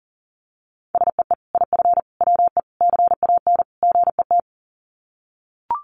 Morse code, heard as a keyed tone, about 40 wpm, 700 Hz. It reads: SEE IF WE CAN GET
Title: Sets of 5 Words Encore2 40wpm